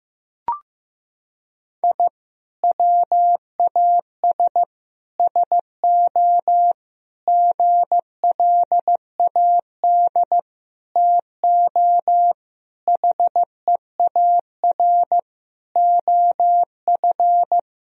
Morse code, heard as a keyed tone, about 15 wpm, 700 Hz. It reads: I WAS SO GLAD TO HEAR OF
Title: Sentences from Top 1000 Words - Review Part 4 - 15wpm